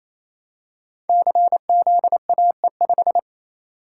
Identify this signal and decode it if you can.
Morse code, heard as a keyed tone, about 27 wpm, 700 Hz. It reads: CZAE5
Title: Code Groups - 5 Letter:Numbers - 28wpm